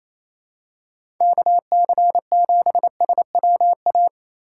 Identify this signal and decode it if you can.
Morse code, heard as a keyed tone, about 28 wpm, 700 Hz. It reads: KC7SWA
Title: USA Call Signs - 6 Characters - 3x Character Spacing - 28wpm